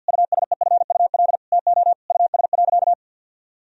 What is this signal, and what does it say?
Morse code, heard as a keyed tone, about 50 wpm, 700 Hz. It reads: USEFUL TO US.
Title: Sentences from Top 1000 Words - Review Part 4 - 50wpm